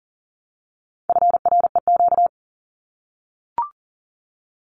Morse code, heard as a keyed tone, about 40 wpm, 700 Hz. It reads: FREQ
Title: Abbreviations Repeated 3x - 40wpm